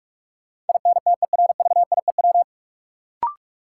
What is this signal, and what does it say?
Morse code, heard as a keyed tone, about 44 wpm, 700 Hz. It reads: INTERVIEW
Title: Sentences from Top 1600 Words - New Word Drill Alt - 45wpm